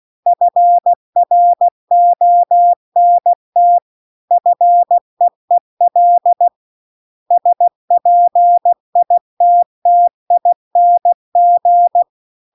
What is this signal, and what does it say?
Morse code, heard as a keyed tone, about 16 wpm, 700 Hz. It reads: FRONT FEEL SPITTING
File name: CW-The-Big-Twoheaded-River-2-16WPM